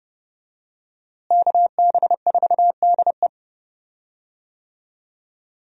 Morse code, heard as a keyed tone, about 32 wpm, 700 Hz. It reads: KB4DE
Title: USA Call Signs - 5 Characters - Repeated 3x - 30wpm